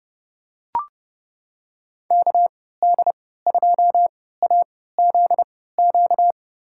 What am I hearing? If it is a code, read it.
Morse code, heard as a keyed tone, about 30 wpm, spaced as for 17 wpm, 700 Hz. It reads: KD2AZQ
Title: USA Call Signs - 6 Characters - 3x Character Spacing - 30wpm